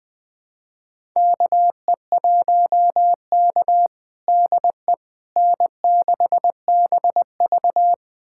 Morse code, heard as a keyed tone, about 20 wpm, 700 Hz. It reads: KE1K DE N6BV
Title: USA Callsign DE Callsign - 4 Characters - 20wpm